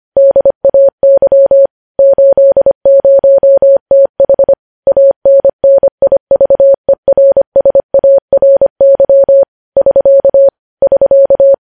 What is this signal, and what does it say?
Morse code, heard as a keyed tone, about 25 wpm, 550 Hz. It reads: DAY 80TH ANNIVERSARY <SK> <SK>
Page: News Headlines in Morse Code at 25 WPM